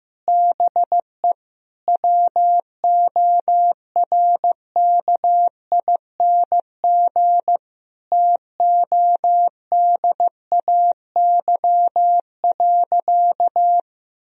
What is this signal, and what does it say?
Morse code, heard as a keyed tone, about 15 wpm, 700 Hz. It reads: BE WORKING TODAY.
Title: Sentences from Top 1000 Words - Review Part 4 - 15wpm